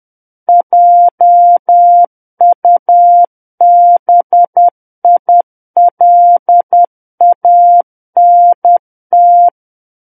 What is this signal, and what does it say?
Morse code, heard as a keyed tone, about 10 wpm, 700 Hz. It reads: JUBILANT
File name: Book One / 01-Chapter 1 - 10 WPM